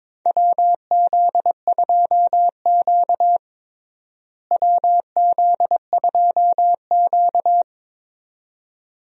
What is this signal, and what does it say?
Morse code, heard as a keyed tone, about 22 wpm, 700 Hz. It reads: WZ2Q WZ2Q
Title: USA Call Signs - 4 Characters - Repeated 2x - 22wpm